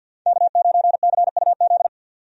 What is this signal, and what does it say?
Morse code, heard as a keyed tone, about 46 wpm, 700 Hz. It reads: K9CUZ
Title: USA Call Signs - 5 Characters - Repeated 2x - 50wpm